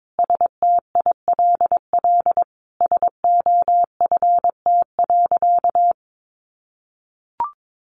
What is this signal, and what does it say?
Morse code, heard as a keyed tone, about 22 wpm, 700 Hz. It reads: STILL SOFT.
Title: Sentences from Top 1500 Words 22wpm